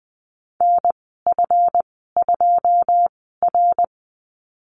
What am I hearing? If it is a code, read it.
Morse code, heard as a keyed tone, about 20 wpm, spaced as for 15 wpm, 700 Hz. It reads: NF2R
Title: 147 Lesson 24g - Letter y - Call Signs 20wpm